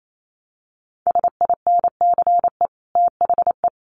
Morse code, heard as a keyed tone, about 28 wpm, 700 Hz. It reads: SINCE THE
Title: Sets of 2 Words 28wpm